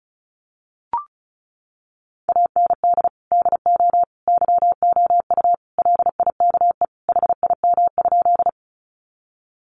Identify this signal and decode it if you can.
Morse code, heard as a keyed tone, about 34 wpm, 700 Hz. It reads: AND DO YOU LIKE HIM?
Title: Sentences from Top 100 Words 35wpm